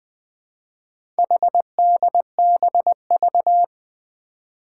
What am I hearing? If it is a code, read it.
Morse code, heard as a keyed tone, about 20 wpm, 700 Hz. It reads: HDBV
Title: Four Challenging Letter-Number Combinations 20wpm